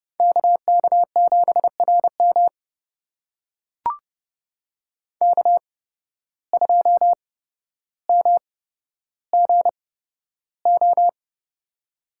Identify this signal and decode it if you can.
Morse code, heard as a keyed tone, about 32 wpm, spaced as for 13 wpm, 700 Hz. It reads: KK7RM K2MGO
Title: USA Call Signs - 5 Characters - 8x Character Spacing - 30wpm